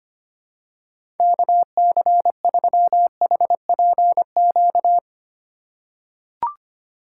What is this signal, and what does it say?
Morse code, heard as a keyed tone, about 25 wpm, 700 Hz. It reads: KC3HPQ
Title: USA Call Signs - 6 Characters - 4x Character Spacing - 25wpm